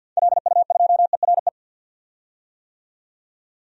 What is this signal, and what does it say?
Morse code, heard as a keyed tone, about 52 wpm, 700 Hz. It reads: LU2ERE
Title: DXCC-100 Call Signs - 6 Characters - 50wpm